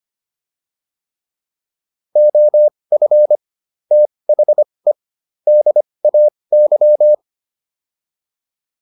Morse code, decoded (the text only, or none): OF THE DAY